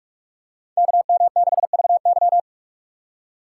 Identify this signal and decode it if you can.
Morse code, heard as a keyed tone, about 45 wpm, 700 Hz. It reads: KM6VY